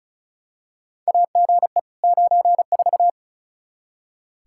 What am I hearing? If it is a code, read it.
Morse code, heard as a keyed tone, about 36 wpm, 700 Hz. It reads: AGE 94
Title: Practice QSO - Exchange 3 - More Info - 35wpm